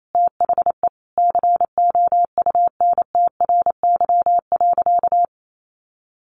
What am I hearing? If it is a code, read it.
Morse code, heard as a keyed tone, about 28 wpm, 700 Hz. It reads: THE COUNTRY.
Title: Sentences from Top 700 Words 28wpm